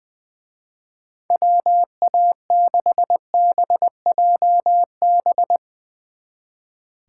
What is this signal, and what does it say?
Morse code, heard as a keyed tone, about 21 wpm, 700 Hz. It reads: WA6BJB
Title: Speed Racing - USA Call Signs 20wpm